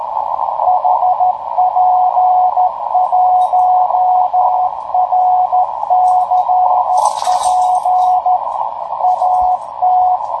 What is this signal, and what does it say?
Morse code, heard as a keyed tone, about 13 wpm, 730 Hz, with radio noise. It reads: HP1RCP AT
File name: tracks / hp1rcp